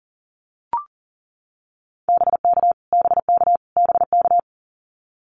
Morse code, heard as a keyed tone, about 40 wpm, 700 Hz. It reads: BK BK BK
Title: Abbreviations Repeated 3x - 40wpm